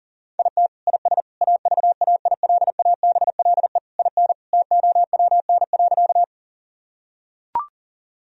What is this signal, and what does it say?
Morse code, heard as a keyed tone, about 40 wpm, 700 Hz. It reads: IT IS AVAILABLE IN TOWN.
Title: Sentences from Top 1000 Words - Review Part 3 - 40wpm